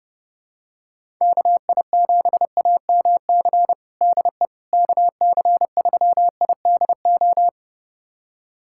Morse code, heard as a keyed tone, about 30 wpm, 700 Hz. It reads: KI7AMC DE KC3IDO
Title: USA Callsign DE Callsign 30wpm